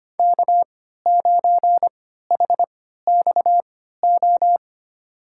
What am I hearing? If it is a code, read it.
Morse code, heard as a keyed tone, about 25 wpm, spaced as for 14 wpm, 700 Hz. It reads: K9HXO
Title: USA Call Signs - 5 Characters - 3x Character Spacing - 25wpm